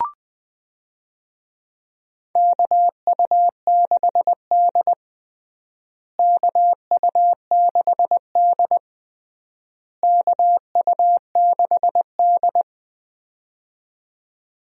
Morse code, heard as a keyed tone, about 20 wpm, 700 Hz. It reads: KU6D KU6D KU6D
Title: USA Call Signs - 4 Characters - Repeated 3x - 20wpm